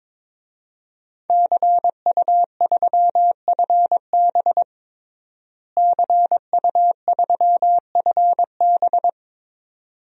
Morse code, heard as a keyed tone, about 22 wpm, 700 Hz. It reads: CU3FB CU3FB
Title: DXCC-100 Call Signs - 5 Characters - Repeated 2x - 22wpm